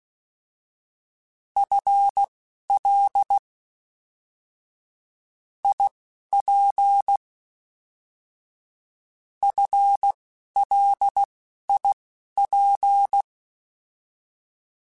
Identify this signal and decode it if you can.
Morse code, heard as a keyed tone, about 16 wpm, spaced as for 11 wpm, 800 Hz. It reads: FL IP FLIP
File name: FL / FL mp3s / Fast FL @10